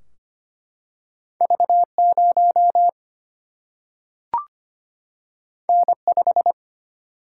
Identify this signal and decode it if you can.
Morse code, heard as a keyed tone, about 25 wpm, 700 Hz. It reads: V0 N5
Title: Single-Letter Single-Number 25wpm